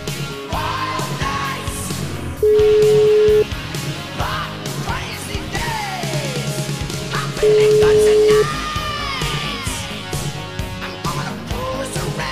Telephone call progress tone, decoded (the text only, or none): ringback tone